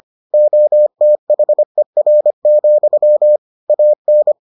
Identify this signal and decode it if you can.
Morse code, heard as a keyed tone, about 25 wpm, 600 Hz. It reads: OTHER, AN